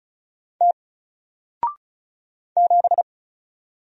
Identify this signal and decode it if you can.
Morse code, heard as a keyed tone, about 36 wpm, 700 Hz. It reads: T 7
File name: Single-Letter-Number-35wpm